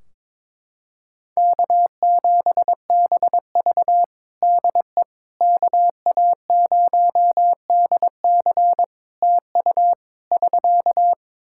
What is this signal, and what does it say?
Morse code, heard as a keyed tone, about 22 wpm, 700 Hz. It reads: K7BV DE KA0DC TU <SK>
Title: Practice QSO - Exchange 4 - Ending - 22wpm